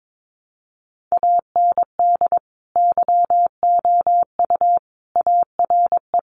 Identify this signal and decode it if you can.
Morse code, heard as a keyed tone, about 22 wpm, 700 Hz. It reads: AND YOU ARE
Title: Sentences from Top 400 Words 22wpm